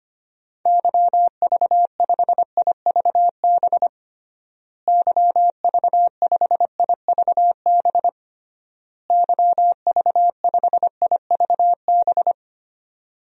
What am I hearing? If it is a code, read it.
Morse code, heard as a keyed tone, about 25 wpm, 700 Hz. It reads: YV5IVB YV5IVB YV5IVB
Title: DXCC-100 Call Signs - 6 Characters - Repeated 3x - 25wpm